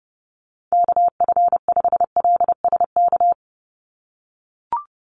Morse code, heard as a keyed tone, about 30 wpm, 700 Hz. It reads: KF5LSK
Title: USA Call Signs 30wpm